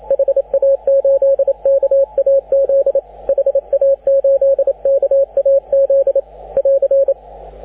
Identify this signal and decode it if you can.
Morse code, heard as a keyed tone, about 29 wpm, 530 Hz, with radio noise. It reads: HA8KAZ HA8KAZ <AR>
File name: AUDIO2019 / HG6N CW SSB MIX